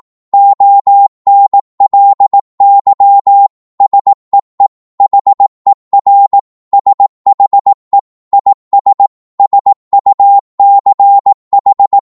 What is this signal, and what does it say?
Morse code, heard as a keyed tone, about 18 wpm, 800 Hz. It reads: ONLY SEE HER SHE IS SUCH
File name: Morse / psg-18wpm